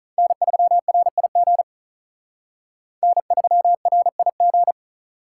Morse code, heard as a keyed tone, about 44 wpm, 700 Hz. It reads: N3RIG N3RIG